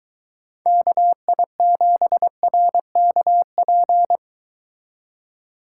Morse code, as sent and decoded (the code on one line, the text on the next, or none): -.- .. --... .-. -.- .--.
KI7RKP